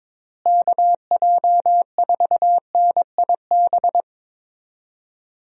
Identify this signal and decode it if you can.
Morse code, heard as a keyed tone, about 22 wpm, 700 Hz. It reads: KJ4NIB
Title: USA Call Signs - 6 Characters - Repeated 3x - 22wpm